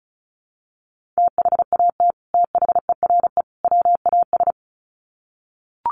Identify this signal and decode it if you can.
Morse code, heard as a keyed tone, about 34 wpm, 700 Hz. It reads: THAT THERE WAS
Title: Sets of 3 Words 35wpm